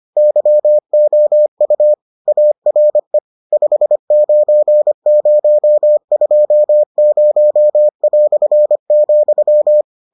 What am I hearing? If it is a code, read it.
Morse code, heard as a keyed tone, about 25 wpm, 600 Hz. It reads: YOU ARE 59020",